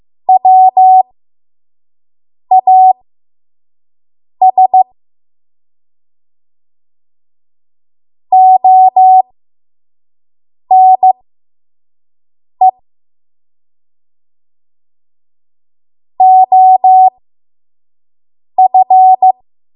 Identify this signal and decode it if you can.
Morse code, heard as a keyed tone, about 15 wpm, spaced as for 5 wpm, 750 Hz. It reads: WAS ONE OF
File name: Morse / 05WPM